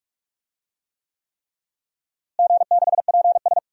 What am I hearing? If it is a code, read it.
Morse code, heard as a keyed tone, about 46 wpm, 700 Hz. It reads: G6PS
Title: DXCC-100 Call Signs - 4 Characters - Repeated 3x - 45wpm